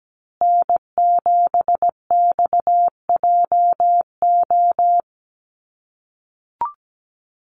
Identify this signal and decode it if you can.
Morse code, heard as a keyed tone, about 17 wpm, 700 Hz. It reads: N7XJO
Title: USA Call Signs Repeated 2x 17wpm